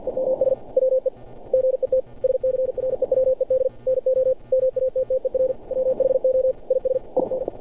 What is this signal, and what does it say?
Morse code, heard as a keyed tone, about 47 wpm, 520 Hz. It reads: EMS WE GET SCREWED NO MATTER WHO IS IN
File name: archived / CRSnet